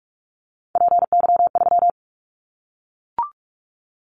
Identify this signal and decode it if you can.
Morse code, heard as a keyed tone, about 44 wpm, 700 Hz. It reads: PY3
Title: Callsign Prefixes for Top 100 DXCC - 45wpm